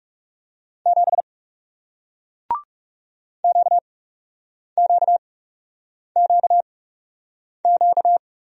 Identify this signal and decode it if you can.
Morse code, heard as a keyed tone, about 38 wpm, spaced as for 8 wpm, 700 Hz. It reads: 7 QQQQ